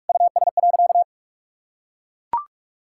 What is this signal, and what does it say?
Morse code, heard as a keyed tone, about 41 wpm, 700 Hz. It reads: US.
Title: Sentences from Top 1000 Words - Review Part 5 - 45wpm